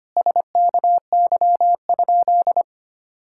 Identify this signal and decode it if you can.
Morse code, heard as a keyed tone, about 25 wpm, 700 Hz. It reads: SKY?
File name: practice / 25wpm / Sentences with 4-Word Phrases - 25wpm